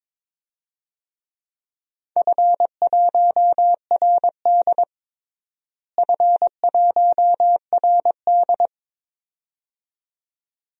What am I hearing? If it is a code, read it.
Morse code, heard as a keyed tone, about 22 wpm, 700 Hz. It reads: F1RD F1RD
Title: DXCC-100 Call Signs - 4 Characters - Repeated 2x - 22wpm